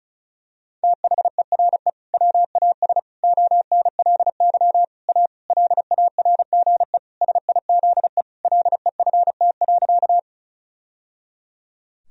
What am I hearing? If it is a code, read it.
Morse code, heard as a keyed tone, about 36 wpm, 700 Hz. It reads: THERE WAS ONLY A LARGE SIZE LEFT.
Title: Sentences from Top 1000 Words - Review Part 2 - 35wpm